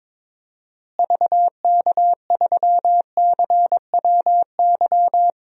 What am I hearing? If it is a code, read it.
Morse code, heard as a keyed tone, about 22 wpm, 700 Hz. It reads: VK3CWY
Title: DXCC-100 Call Signs - 6 Characters - Repeated 2x - 22wpm